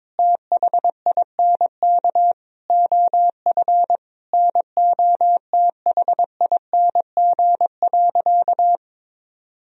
Morse code, heard as a keyed tone, about 22 wpm, 700 Hz. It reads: THINK OF NOTHING.
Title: Sentences from Top 1000 Words - Review Part 3 - 22wpm